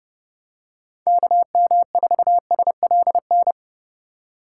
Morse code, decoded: KM4SLN

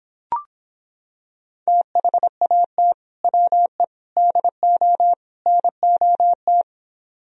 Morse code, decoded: THAT WE DO NOT

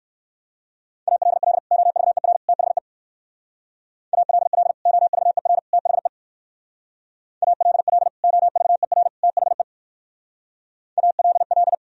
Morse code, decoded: ALL OVER THE ALL OVER THE ALL OVER THE ALL

